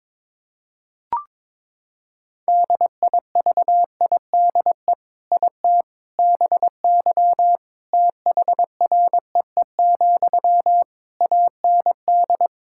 Morse code, heard as a keyed tone, about 22 wpm, 700 Hz. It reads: DIVIDE IT BY THREE, AND
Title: Sentences from Top 1600 Words 22wpm